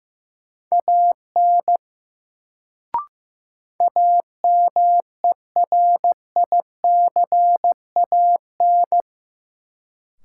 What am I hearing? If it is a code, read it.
Morse code, heard as a keyed tone, about 15 wpm, 700 Hz. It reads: AN AMERICAN
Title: Sets of 2 Words - New Word Drill - 15wpm